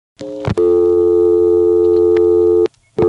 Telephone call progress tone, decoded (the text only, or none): dial tone